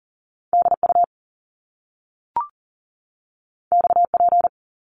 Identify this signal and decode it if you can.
Morse code, heard as a keyed tone, about 40 wpm, 700 Hz. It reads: DU XP
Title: Top Two Letter Combinations in Words 40wpm